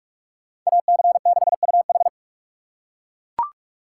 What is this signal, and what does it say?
Morse code, heard as a keyed tone, about 40 wpm, 700 Hz. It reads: AC6UH